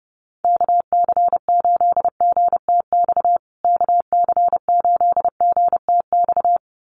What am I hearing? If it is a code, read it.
Morse code, heard as a keyed tone, about 30 wpm, 700 Hz. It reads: KC8GTX KC8GTX
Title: USA Call Signs Repeated 2x 30wpm